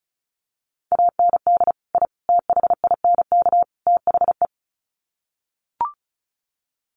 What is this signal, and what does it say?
Morse code, heard as a keyed tone, about 34 wpm, 700 Hz. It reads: AND I THINK THE
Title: Sets of 4 Words 35wpm